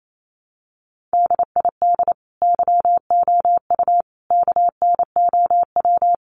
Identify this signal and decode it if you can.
Morse code, heard as a keyed tone, about 28 wpm, 700 Hz. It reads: DID YOU KNOW
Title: Sentences from Top 100 Words 28wpm